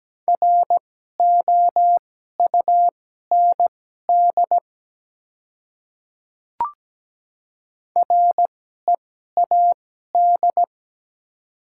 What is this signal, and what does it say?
Morse code, heard as a keyed tone, about 17 wpm, spaced as for 12 wpm, 700 Hz. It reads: ROUND READ
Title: 2x Character Spacing - Top 200 Words - 17wpm